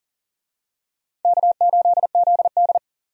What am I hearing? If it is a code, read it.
Morse code, heard as a keyed tone, about 40 wpm, 700 Hz. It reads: K8ZD
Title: USA Call Signs - 4 Characters - 40wpm